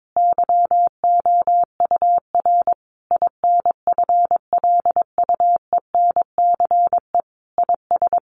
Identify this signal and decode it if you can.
Morse code, heard as a keyed tone, about 22 wpm, 700 Hz. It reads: YOUR INFLUENCE IS
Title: Sentences from Top 1100 Words 22wpm